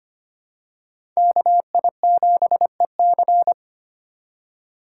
Morse code, heard as a keyed tone, about 25 wpm, 700 Hz. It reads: KI7EC
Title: USA Call Signs - 5 Characters - 4x Character Spacing - 25wpm